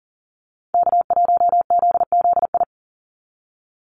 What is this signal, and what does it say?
Morse code, heard as a keyed tone, about 40 wpm, 700 Hz. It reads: K1ZZI
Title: Call Signs - 40wpm